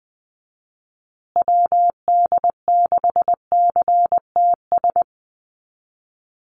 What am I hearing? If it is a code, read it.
Morse code, heard as a keyed tone, about 20 wpm, 700 Hz. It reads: WD6CTS
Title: USA Call Signs Repeated 3x 20wpm